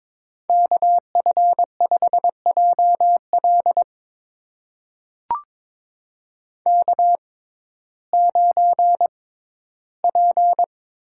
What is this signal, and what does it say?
Morse code, heard as a keyed tone, about 22 wpm, spaced as for 13 wpm, 700 Hz. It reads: KF5JL K9P
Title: USA Call Signs - 5 Characters - 6x Character Spacing - 22wpm